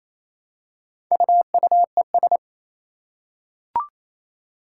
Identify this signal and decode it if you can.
Morse code, heard as a keyed tone, about 27 wpm, 700 Hz. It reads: UUES
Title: Four Challenging Letter-Number Combinations 28wpm